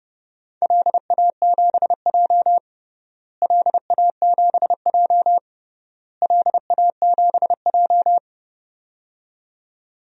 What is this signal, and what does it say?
Morse code, heard as a keyed tone, about 32 wpm, 700 Hz. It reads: LA7J LA7J LA7J
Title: DXCC-100 Call Signs - 4 Characters - Repeated 3x - 30wpm